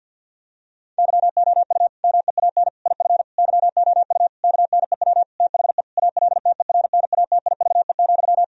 Morse code, heard as a keyed tone, about 52 wpm, 700 Hz. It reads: YOU MEAN IF YOU KNEW THE ALTERNATIVE,
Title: Sentences from Top 1000 Words - Review Part 4 - 50wpm